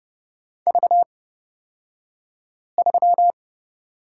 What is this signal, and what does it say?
Morse code, heard as a keyed tone, about 31 wpm, spaced as for 5 wpm, 700 Hz. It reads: V3
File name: Lesson 38d - Letter q - Two Characters 30wpm